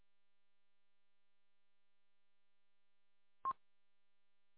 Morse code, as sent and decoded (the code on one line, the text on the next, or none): none